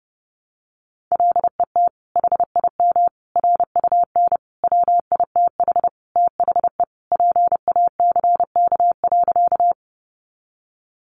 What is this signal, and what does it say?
Morse code, heard as a keyed tone, about 30 wpm, 700 Hz. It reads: LET HIM RUN WITH THE PACK.
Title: Sentences from Top 1300 Words 30wpm